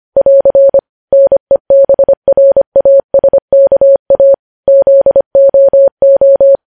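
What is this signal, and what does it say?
Morse code, heard as a keyed tone, about 25 wpm, 550 Hz. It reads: <AR> NEBRASKA ZOO
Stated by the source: Morse code news headlines at 25 words per minute.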